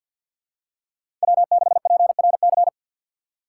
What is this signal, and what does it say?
Morse code, heard as a keyed tone, about 52 wpm, 700 Hz. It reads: W6PRC
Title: USA Call Signs - 2x Character Spacing - 50wpm